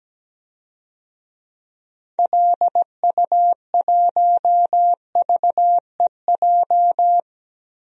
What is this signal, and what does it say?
Morse code, heard as a keyed tone, about 17 wpm, 700 Hz. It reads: LU1VEJ
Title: DXCC-100 Call Signs - 6 Characters - Repeated 3x - 17wpm